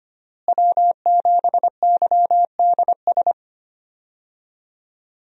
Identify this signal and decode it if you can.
Morse code, heard as a keyed tone, about 25 wpm, 700 Hz. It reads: W7YDS
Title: USA Call Signs - 5 Characters - Repeated 3x - 25wpm